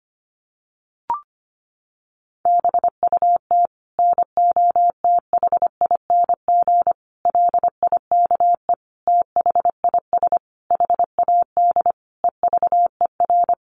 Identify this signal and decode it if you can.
Morse code, heard as a keyed tone, about 25 wpm, 700 Hz. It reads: BUT NOTHING LIKE THIS HAD EVER
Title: Sentences from Top 1100 Words 25wpm